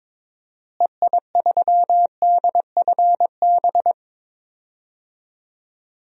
Morse code, heard as a keyed tone, about 22 wpm, 700 Hz. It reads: EI3DFB
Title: DXCC-100 Call Signs - 6 Characters - Repeated 3x - 22wpm